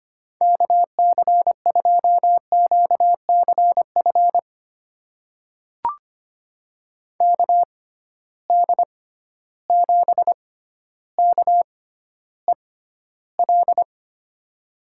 Morse code, heard as a keyed tone, about 25 wpm, spaced as for 13 wpm, 700 Hz. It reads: KC2QCF KD7KEL
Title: USA Call Signs - 6 Characters - 6x Character Spacing - 25wpm